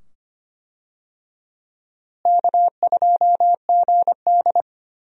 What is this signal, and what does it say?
Morse code, heard as a keyed tone, about 25 wpm, 700 Hz. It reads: K2GD
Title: USA Call Signs - 4 Characters - Repeated 2x - 25wpm